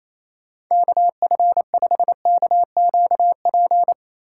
Morse code, heard as a keyed tone, about 28 wpm, 700 Hz. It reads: KF5KQP
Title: USA Call Signs - 6 Characters - 6x Character Spacing - 28wpm